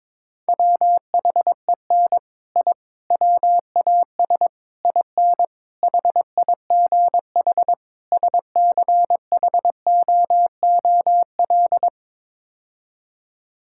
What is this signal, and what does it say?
Morse code, heard as a keyed tone, about 22 wpm, 700 Hz. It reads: WHEN I WAS IN HIGH SCHOOL
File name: practice / 22wpm / Sets of 6 Words - 22wpm